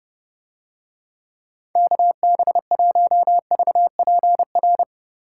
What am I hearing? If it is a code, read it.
Morse code, heard as a keyed tone, about 31 wpm, 700 Hz. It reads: KB1VPR